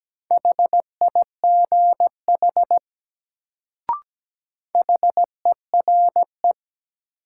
Morse code, heard as a keyed tone, about 17 wpm, 700 Hz. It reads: HIGH HERE
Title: Sentences from Top 200 Words - New Word Drill - 17wpm